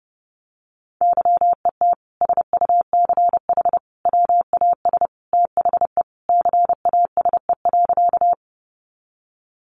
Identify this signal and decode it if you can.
Morse code, heard as a keyed tone, about 30 wpm, 700 Hz. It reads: YET SUCH WAS THE CASE.
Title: Sentences from Top 700 Words 30wpm